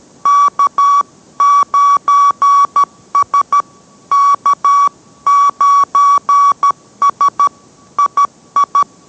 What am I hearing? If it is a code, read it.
Morse code, heard as a keyed tone, about 13 wpm, 1140 Hz. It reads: K9S K9S II